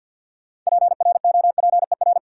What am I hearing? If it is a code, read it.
Morse code, heard as a keyed tone, about 51 wpm, 700 Hz. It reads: PROPER
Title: Sentences from Top 800 Words - New Word Drill - 50wpm